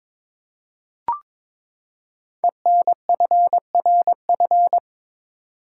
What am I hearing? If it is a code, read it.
Morse code, heard as a keyed tone, about 22 wpm, 700 Hz. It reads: ENFRF
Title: Code Groups - 5 Letters - 22wpm